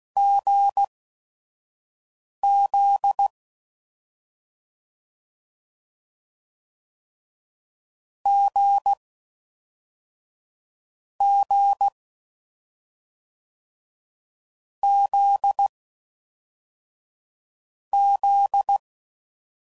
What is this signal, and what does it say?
Morse code, heard as a keyed tone, about 16 wpm, spaced as for 5 wpm, 800 Hz. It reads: GZ G G Z Z